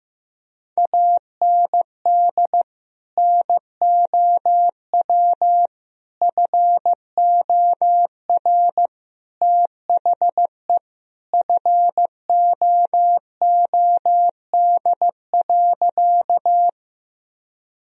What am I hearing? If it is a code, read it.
Morse code, heard as a keyed tone, about 15 wpm, 700 Hz. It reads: AND NOW FOR THE FOOD.
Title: Sentences from Top 1000 Words - Review Part 1 - 15wpm